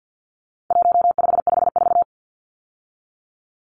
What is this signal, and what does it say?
Morse code, heard as a keyed tone, about 50 wpm, 700 Hz. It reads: 1554
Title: Numbers - 4-Digit 50wpm